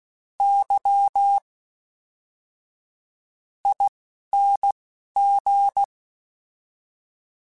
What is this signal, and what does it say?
Morse code, heard as a keyed tone, about 16 wpm, spaced as for 7 wpm, 800 Hz. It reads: YING
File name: Special INGs / Special INGs-mp3s / Fast ing @10